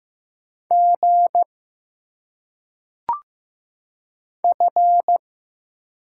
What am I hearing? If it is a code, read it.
Morse code, heard as a keyed tone, about 15 wpm, 700 Hz. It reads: G F
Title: Single Letters - Phonetic Pronunciation 15wpm